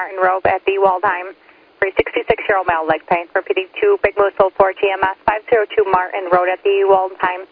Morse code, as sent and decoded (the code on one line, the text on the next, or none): none